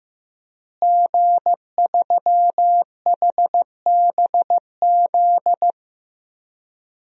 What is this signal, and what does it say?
Morse code, heard as a keyed tone, about 16 wpm, 700 Hz. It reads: G3HBZ
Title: DXCC-100 Call Signs - 5 Characters - Repeated 2x - 15wpm